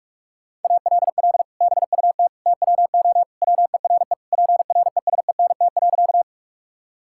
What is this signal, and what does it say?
Morse code, heard as a keyed tone, about 43 wpm, 700 Hz. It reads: ALL BUT TWO WERE PRESENT.
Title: Sentences from Top 1000 Words - Review Part 5 - 45wpm